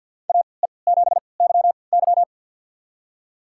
Morse code, heard as a keyed tone, about 52 wpm, spaced as for 29 wpm, 700 Hz. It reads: AE7YY
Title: USA Call Signs - 5 Characters - 3x Character Spacing - 50wpm